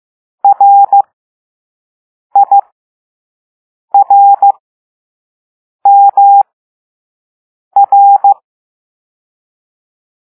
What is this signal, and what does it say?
Morse code, heard as a keyed tone, about 15 wpm, spaced as for 5 wpm, 800 Hz. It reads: RIRMR
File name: Morse / Level2A